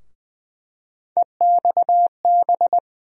Morse code, heard as a keyed tone, about 20 wpm, 700 Hz. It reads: EXB
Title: USA Call Sign Postfixes - 3 Letters 20wpm